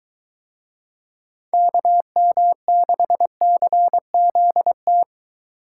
Morse code, heard as a keyed tone, about 23 wpm, 700 Hz. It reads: KM6CZT